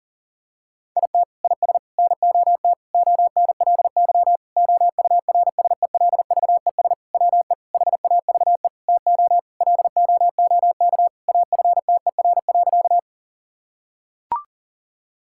Morse code, heard as a keyed tone, about 40 wpm, 700 Hz. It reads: IT IS NOT ONLY OURSELVES WE HAVE TO LOOK AFTER.
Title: Sentences from Top 1600 Words 40wpm